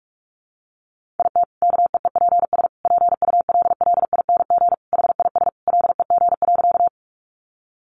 Morse code, heard as a keyed tone, about 44 wpm, 700 Hz. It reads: IT KEEPS PULLING HIS LEG.
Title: Additional Easy Phrases 45wpm